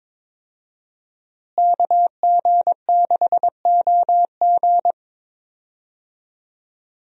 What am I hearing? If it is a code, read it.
Morse code, heard as a keyed tone, about 22 wpm, 700 Hz. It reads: KG6OG
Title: USA Call Signs - 5 Characters - Repeated 2x - 22wpm